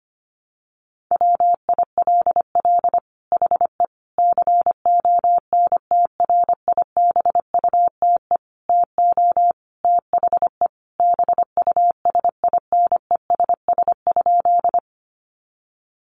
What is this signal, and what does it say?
Morse code, heard as a keyed tone, about 25 wpm, 700 Hz. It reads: WILL HE CONTRIBUTE TO THE BUSINESS?
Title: Sentences from Top 1400 Words 25wpm